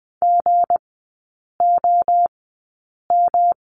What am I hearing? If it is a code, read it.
Morse code, heard as a keyed tone, about 20 wpm, spaced as for 8 wpm, 690 Hz. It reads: GOM
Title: 159 Lesson 26e - Letter g - Three Characters 20wpm